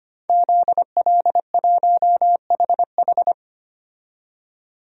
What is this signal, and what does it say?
Morse code, heard as a keyed tone, about 25 wpm, 700 Hz. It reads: ZL1HH
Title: DXCC-100 Call Signs - 5 Characters - 25wpm